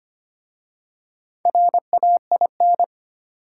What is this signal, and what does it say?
Morse code, heard as a keyed tone, about 25 wpm, 700 Hz. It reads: RAIN
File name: practice / 25wpm / Easy Words with Hints - 25wpm